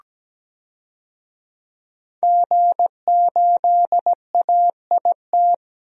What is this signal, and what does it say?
Morse code, heard as a keyed tone, about 17 wpm, 700 Hz. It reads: G8AIT
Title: DXCC-100 Call Signs - 5 Characters - Repeated 3x - 17wpm